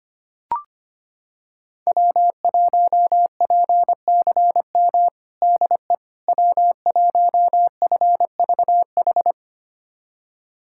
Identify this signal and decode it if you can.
Morse code, heard as a keyed tone, about 25 wpm, 700 Hz. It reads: W1PCM DE W1FVH
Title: USA Callsign DE Callsign - 5 Characters - 25wpm